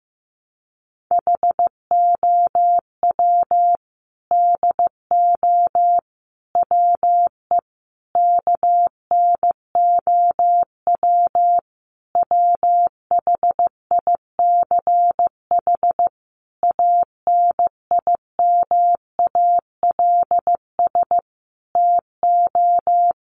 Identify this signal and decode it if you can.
Morse code, heard as a keyed tone, about 15 wpm, 700 Hz. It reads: HOW DO WE KNOW WHICH ANIMALS TO
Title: A Sound Of Thunder 15wpm